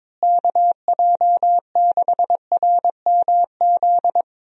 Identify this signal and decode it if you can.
Morse code, heard as a keyed tone, about 22 wpm, 700 Hz. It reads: KJ6RMZ
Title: USA Call Signs - 6 Characters - Repeated 3x - 22wpm